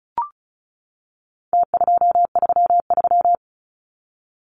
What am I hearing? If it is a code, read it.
Morse code, heard as a keyed tone, about 34 wpm, 700 Hz. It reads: T233
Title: Numbers - Contest Style - 4-Digit 35wpm